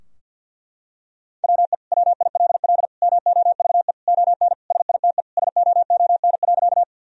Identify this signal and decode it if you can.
Morse code, heard as a keyed tone, about 50 wpm, 700 Hz. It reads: WE WILL MOVE ON SITE SOON.
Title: Sentences from Top 1000 Words - Review Part 4 - 50wpm